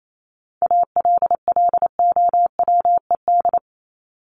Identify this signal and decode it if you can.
Morse code, heard as a keyed tone, about 28 wpm, 700 Hz. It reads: ALLOWED
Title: ED Suffix - Top 100 Words with Base Words 28wpm